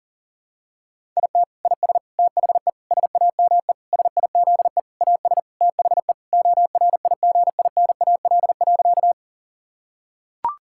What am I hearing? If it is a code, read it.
Morse code, heard as a keyed tone, about 40 wpm, 700 Hz. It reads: IT IS THE SAME SIZE AS THE ORIGINAL.
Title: Sentences from Top 1000 Words - Review Part 6 - 40wpm